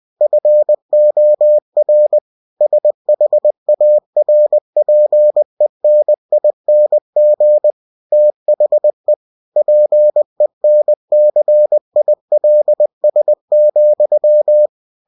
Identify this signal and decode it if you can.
Morse code, heard as a keyed tone, about 20 wpm, 600 Hz. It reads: FOR SHARPENING THE PENCILS,
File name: cw-books / winnie-the-pooh / winnie-20wpm-page-0109